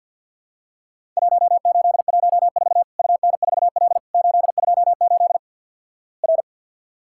Morse code, heard as a keyed tone, about 50 wpm, 700 Hz. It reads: 1814 UN4L 828 R
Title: Worked All Europe DX Contest - QTC Exchange with Repeats - 50wpm